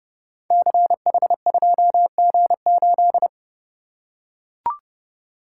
Morse code, heard as a keyed tone, about 30 wpm, 700 Hz. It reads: CH2G8
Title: Code Groups - 5 Letter:Numbers - 30wpm